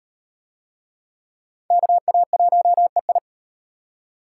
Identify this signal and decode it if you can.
Morse code, heard as a keyed tone, about 38 wpm, 700 Hz. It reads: KA1EI